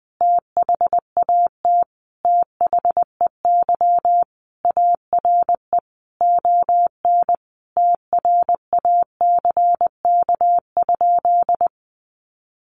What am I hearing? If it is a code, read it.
Morse code, heard as a keyed tone, about 20 wpm, 700 Hz. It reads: THAT THEY ARE ON TRACK?
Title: Sentences from Top 1200 Words 20wpm